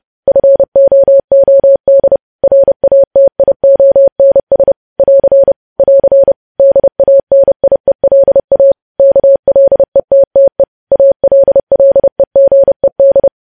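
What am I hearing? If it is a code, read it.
Morse code, heard as a keyed tone, about 30 wpm, 550 Hz. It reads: FOOD RATIONS <AR> <AR> DANIELA KLETTE ALLEGED
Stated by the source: Morse code news headlines sent at 30, preceded by vvv vvv.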